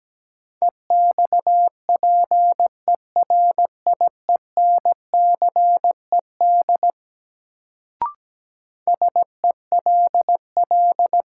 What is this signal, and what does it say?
Morse code, heard as a keyed tone, about 17 wpm, 700 Hz. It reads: EXPERIENCED SELL
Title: Sentences from Top 1100 Words - New Word Drill Alt - 17wpm